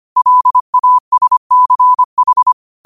Morse code, heard as a keyed tone, about 25 wpm, 1000 Hz. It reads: LASCH
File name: qotdfiles / 25wpm